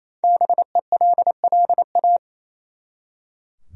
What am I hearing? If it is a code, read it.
Morse code, heard as a keyed tone, about 28 wpm, 700 Hz. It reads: BELLA
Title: USA Common Names - Max 5 Letters - 28wpm